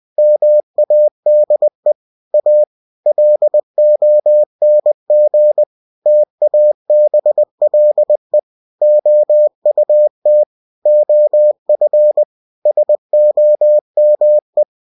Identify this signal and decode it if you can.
Morse code, heard as a keyed tone, about 20 wpm, 600 Hz. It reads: MADE A LONG TABLE OUT OF SOME
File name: cw-books / winnie-the-pooh / winnie-20wpm-page-0106